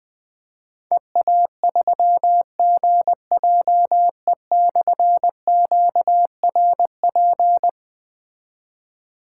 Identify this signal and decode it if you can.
Morse code, heard as a keyed tone, about 21 wpm, 700 Hz. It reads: EA3GJE/QRP
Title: Difficult Call Signs 20wpm